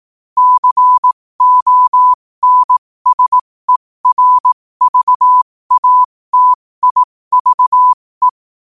Morse code, heard as a keyed tone, about 18 wpm, spaced as for 15 wpm, 1000 Hz. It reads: CONSERVATIVE
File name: qotdfiles / 15wpm